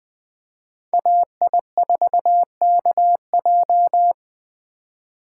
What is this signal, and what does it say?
Morse code, heard as a keyed tone, about 20 wpm, 700 Hz. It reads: AI4KJ
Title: USA Call Signs - 5 Characters - 8x Character Spacing - 20wpm